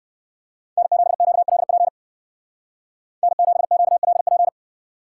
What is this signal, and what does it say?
Morse code, heard as a keyed tone, about 59 wpm, 700 Hz. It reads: N7QLP N7QLP